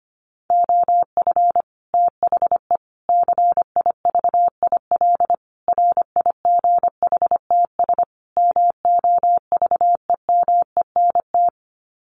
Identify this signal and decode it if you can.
Morse code, heard as a keyed tone, about 25 wpm, 700 Hz. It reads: OF THE CIVIL RIGHTS MOVEMENT
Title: Sets of 5 Words Encore2 25wpm